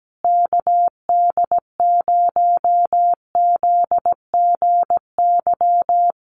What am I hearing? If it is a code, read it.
Morse code, heard as a keyed tone, about 17 wpm, 700 Hz. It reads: KD0ZGY
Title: USA Call Signs Repeated 3x 17wpm